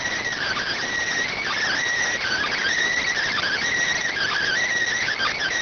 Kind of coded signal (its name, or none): Emergency Alert System (SAME) tones